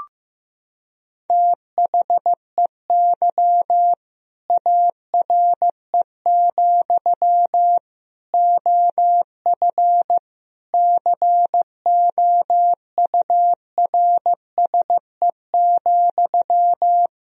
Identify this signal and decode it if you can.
Morse code, heard as a keyed tone, about 15 wpm, 700 Hz. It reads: THEY ARE, OF COURSE,
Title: Sentences from Top 1000 Words - Review Part 4 - 15wpm